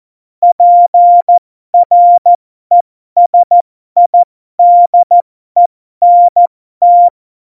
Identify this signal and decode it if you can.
Morse code, heard as a keyed tone, about 14 wpm, spaced as for 12 wpm, 700 Hz. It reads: PRESIDENT
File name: Morsekurs / Witze